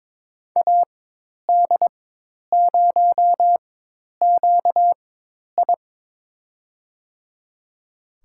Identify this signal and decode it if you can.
Morse code, heard as a keyed tone, about 22 wpm, spaced as for 10 wpm, 700 Hz. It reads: AD0QI
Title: USA Call Signs - 5 Characters - 4x Character Spacing - 22wpm